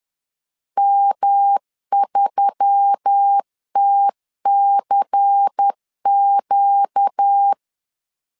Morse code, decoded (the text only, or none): M3TCQ